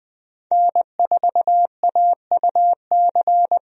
Morse code, heard as a keyed tone, about 20 wpm, 700 Hz. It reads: N4AUC
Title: USA Call Signs - 5 Characters - Repeated 2x - 20wpm